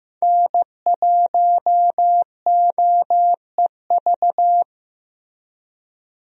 Morse code, heard as a keyed tone, about 15 wpm, 700 Hz. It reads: N1OEV
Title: USA Call Signs - 5 Characters - Repeated 3x - 15wpm